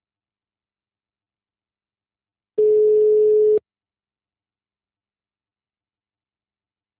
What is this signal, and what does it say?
Ringback tone (Europe, 425 Hz).
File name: out- -2007- - - .8473